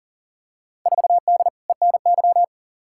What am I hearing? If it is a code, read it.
Morse code, heard as a keyed tone, about 40 wpm, 700 Hz. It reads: 4D ENY
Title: Field Day Exchanges Encore - 40wpm